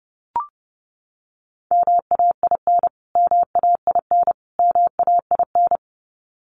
Morse code, heard as a keyed tone, about 30 wpm, 700 Hz. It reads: MAIN MAIN MAIN
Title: Top 500 Words Repeated 30wpm